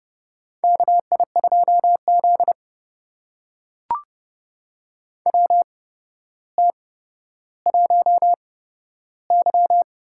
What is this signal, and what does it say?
Morse code, heard as a keyed tone, about 30 wpm, spaced as for 13 wpm, 700 Hz. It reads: KI2Z WT1Y
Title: USA Call Signs - 4 Characters - 8x Character Spacing - 30wpm